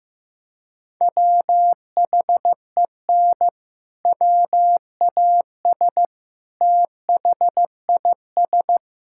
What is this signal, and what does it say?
Morse code, heard as a keyed tone, about 15 wpm, 700 Hz. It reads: WHEN WAS THIS
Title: Sentences from Top 1600 Words 15wpm